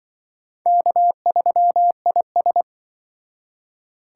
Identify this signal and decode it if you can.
Morse code, heard as a keyed tone, about 24 wpm, 700 Hz. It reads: K3IS
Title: Speed Racing - USA 1x2 Call Signs 20wpm